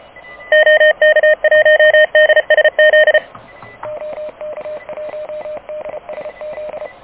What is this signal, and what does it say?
Morse code, heard as a keyed tone, about 30 wpm, 610 Hz. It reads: OK1DSZ OK1DSZ
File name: truSDX / KVPA ruSDXC